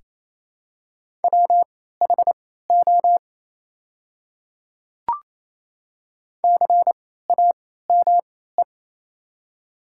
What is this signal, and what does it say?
Morse code, heard as a keyed tone, about 28 wpm, spaced as for 16 wpm, 700 Hz. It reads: WHO CAME
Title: 3x Character Spacing - Top 200 Words - 28wpm